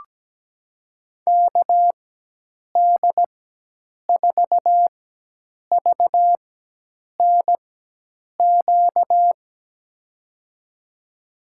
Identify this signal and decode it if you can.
Morse code, heard as a keyed tone, about 17 wpm, spaced as for 8 wpm, 700 Hz. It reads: KD4VNQ
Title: USA Call Signs - 6 Characters - 4x Character Spacing - 17wpm